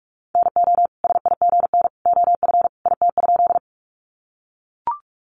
Morse code, heard as a keyed tone, about 44 wpm, 700 Hz. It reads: NO SIGN OF IT?
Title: Sentences from Top 1000 Words 45wpm